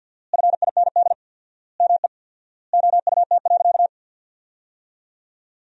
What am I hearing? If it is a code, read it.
Morse code, heard as a keyed tone, about 50 wpm, 700 Hz. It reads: FIND ME OUT.
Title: 4x Word-Spacing - Sentences from Top 500 Words - 50wpm